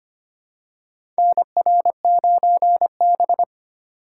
Morse code, heard as a keyed tone, about 25 wpm, 700 Hz. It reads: NR9B
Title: USA Call Signs - 4 Characters - 3x Character Spacing - 25wpm